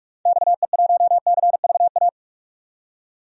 Morse code, heard as a keyed tone, about 46 wpm, 700 Hz. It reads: XE1CVA
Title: DXCC-100 Call Signs - 6 Characters - 45wpm